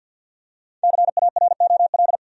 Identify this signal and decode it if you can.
Morse code, heard as a keyed tone, about 47 wpm, 700 Hz. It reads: CAROL
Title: USA Common Names - Max 5 Letters - 50wpm